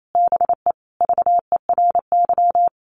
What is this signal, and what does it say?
Morse code, heard as a keyed tone, about 28 wpm, 700 Hz. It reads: BE VERY
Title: Sentences from Top 1400 Words 28wpm